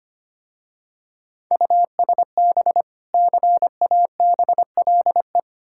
Morse code, heard as a keyed tone, about 25 wpm, 700 Hz. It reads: USB CABLE
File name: practice / 25wpm / Difficult Words with Hints - 25wpm